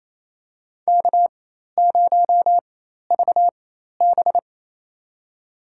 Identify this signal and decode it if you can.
Morse code, heard as a keyed tone, about 28 wpm, spaced as for 13 wpm, 700 Hz. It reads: K0VB
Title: USA Call Signs - 4 Characters - 4x Character Spacing - 28wpm